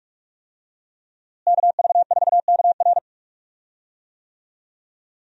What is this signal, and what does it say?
Morse code, heard as a keyed tone, about 43 wpm, 700 Hz. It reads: KV4KR
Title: USA Call Signs - 5 Characters - Repeated 3x - 45wpm